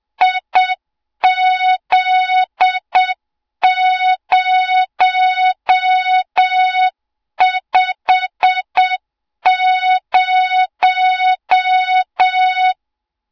Morse code, heard as a keyed tone, about 7 wpm, 3700 Hz, with noise. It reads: IZ050
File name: Morze-HA2MN / hang / 20 GYAKORLAT